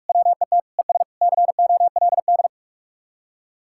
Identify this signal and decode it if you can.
Morse code, heard as a keyed tone, about 47 wpm, 700 Hz. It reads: WET ES COLD